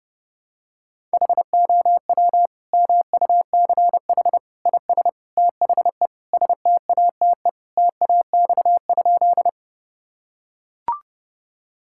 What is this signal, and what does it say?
Morse code, heard as a keyed tone, about 30 wpm, 700 Hz. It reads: HOW MUCH IS THE STATE TAX?
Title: Sentences from Top 1000 Words - Review Part 5 - 30wpm